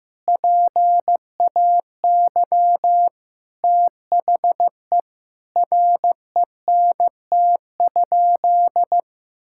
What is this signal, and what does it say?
Morse code, heard as a keyed tone, about 15 wpm, 700 Hz. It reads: PAY THE RENT?
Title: Sentences from Top 1000 Words - Review Part 5 - 15wpm